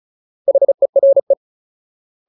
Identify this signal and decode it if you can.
Morse code, heard as a keyed tone, about 34 wpm, 550 Hz. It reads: HERE
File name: Ultimate Word List / 35WPM Ultimate